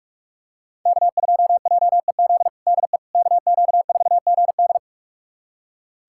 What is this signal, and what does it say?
Morse code, heard as a keyed tone, about 41 wpm, 700 Hz. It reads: K2JEZ DE KQ4GD
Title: USA Callsign DE Callsign - 5 Characters - 45wpm